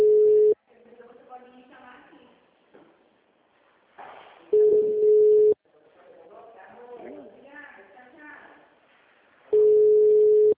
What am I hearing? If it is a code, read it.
Ringback tone (Europe, 425 Hz), with speech.
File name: out- -2005- - - .15590